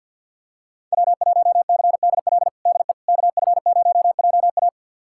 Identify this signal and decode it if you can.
Morse code, heard as a keyed tone, about 51 wpm, 700 Hz. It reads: W1CDL DE KF0JA
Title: USA Callsign DE Callsign - 5 Characters - 50wpm